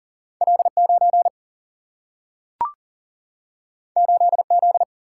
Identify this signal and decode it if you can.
Morse code, heard as a keyed tone, about 40 wpm, 700 Hz. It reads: L9 8Z
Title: Single-Letter Single-Number 40wpm